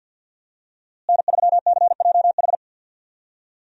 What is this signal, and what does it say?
Morse code, heard as a keyed tone, about 50 wpm, 700 Hz. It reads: N3CJH
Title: USA Call Signs - 5 Characters - Repeated 2x - 50wpm